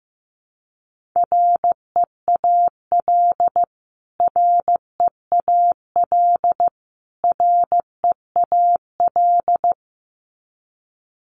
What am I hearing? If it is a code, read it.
Morse code, heard as a keyed tone, about 15 wpm, 700 Hz. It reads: REAL REAL REAL
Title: Top 500 Words Repeated 15wpm